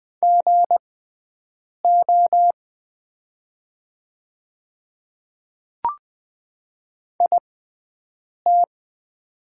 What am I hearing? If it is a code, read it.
Morse code, heard as a keyed tone, about 20 wpm, spaced as for 7 wpm, 700 Hz. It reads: GO IT
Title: 6x Character Spacing - Top 200 Words - 20wpm